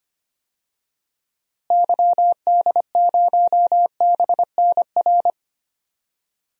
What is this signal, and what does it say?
Morse code, heard as a keyed tone, about 25 wpm, 700 Hz. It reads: YD0BNR
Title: DXCC-100 Call Signs - 6 Characters - Repeated 3x - 25wpm